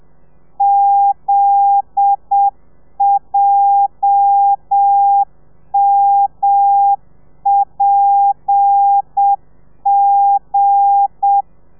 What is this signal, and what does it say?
Morse code, heard as a keyed tone, about 7 wpm, 800 Hz. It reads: ZJMPG